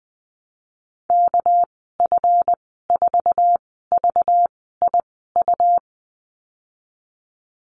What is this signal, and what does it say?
Morse code, heard as a keyed tone, about 20 wpm, spaced as for 15 wpm, 700 Hz. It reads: KF4VIU
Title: 238 Lesson 38g - Letter q - Call Signs 20wpm